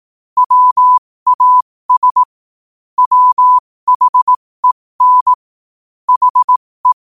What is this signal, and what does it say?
Morse code, heard as a keyed tone, about 18 wpm, spaced as for 15 wpm, 1000 Hz. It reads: WAS WHEN HE
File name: qotdfiles / 15wpm